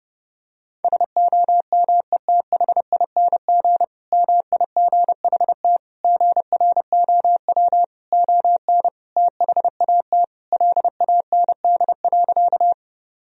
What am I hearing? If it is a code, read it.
Morse code, heard as a keyed tone, about 30 wpm, 700 Hz. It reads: SOMETHING MIGHT GROW ON THAT LAND.
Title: Sentences from Top 1000 Words - Review Part 1 - 30wpm